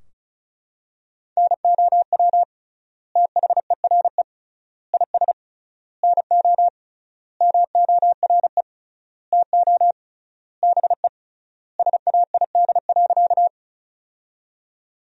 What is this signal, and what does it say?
Morse code, heard as a keyed tone, about 34 wpm, 700 Hz. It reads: NOW THERE IS NO MORE TO BE SAID.
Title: 3x Word-Spacing - Sentences from Top 100 Words - 35wpm